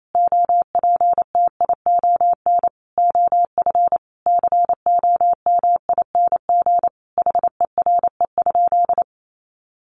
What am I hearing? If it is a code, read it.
Morse code, heard as a keyed tone, about 28 wpm, 700 Hz. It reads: OPTION OF COMING HERE?
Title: Sentences from Top 1300 Words 28wpm